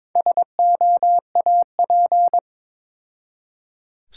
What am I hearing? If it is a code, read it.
Morse code, heard as a keyed tone, about 22 wpm, 690 Hz. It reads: SOAP
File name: Difficult Words with Hints - 22wpm